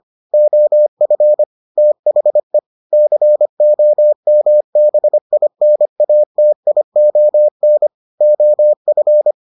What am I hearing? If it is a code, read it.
Morse code, heard as a keyed tone, about 25 wpm, 600 Hz. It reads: OF THE COMBINATION OF